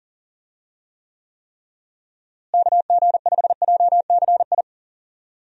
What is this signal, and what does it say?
Morse code, heard as a keyed tone, about 40 wpm, 700 Hz. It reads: KG5JCI
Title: USA Call Signs - 6 Characters - Repeated 3x - 40wpm